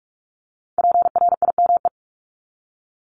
Morse code, heard as a keyed tone, about 46 wpm, 700 Hz. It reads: PRIME
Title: Multi-letter Phonemes 45wpm